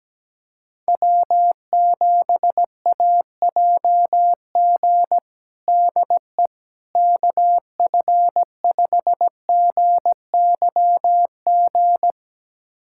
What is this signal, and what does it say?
Morse code, heard as a keyed tone, about 17 wpm, 700 Hz. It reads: W7AJG DE KF5GYG
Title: USA Callsign DE Callsign 17wpm